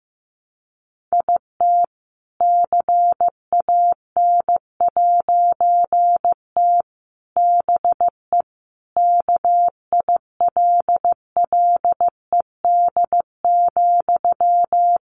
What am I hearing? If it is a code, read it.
Morse code, heard as a keyed tone, about 15 wpm, 700 Hz. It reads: IT CAN'T BE KILLED,
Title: A Sound Of Thunder 15wpm